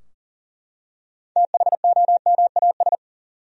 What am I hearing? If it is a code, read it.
Morse code, heard as a keyed tone, about 40 wpm, 700 Hz. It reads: THOMAS
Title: USA Common Names - 40wpm